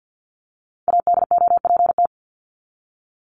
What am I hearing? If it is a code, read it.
Morse code, heard as a keyed tone, about 50 wpm, 700 Hz. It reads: ADOPT
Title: ED Suffix - Top 100 Words with Base Words 50wpm